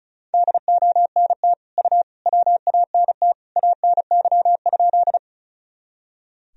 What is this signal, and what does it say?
Morse code, heard as a keyed tone, about 36 wpm, 700 Hz. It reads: DONT U WANT ANY?
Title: Sentences from Top 300 Words and Abbreviations 35wpm